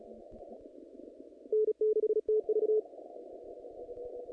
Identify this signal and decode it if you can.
Morse code, heard as a keyed tone, about 36 wpm, 430 Hz. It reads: N6TV
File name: contests / naqp-cw-k5tr-op-k5pi- -2300